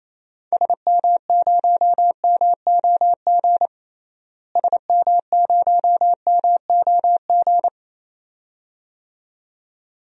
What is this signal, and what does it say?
Morse code, heard as a keyed tone, about 28 wpm, 700 Hz. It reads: SM0MOG SM0MOG
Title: DXCC-100 Call Signs - 6 Characters - Repeated 2x - 28wpm